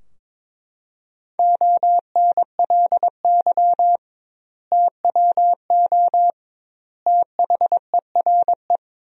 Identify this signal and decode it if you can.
Morse code, heard as a keyed tone, about 22 wpm, 700 Hz. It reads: ONLY TWO THERE
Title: 2x Word-Spacing - Sentences from Top 200 Words - 22wpm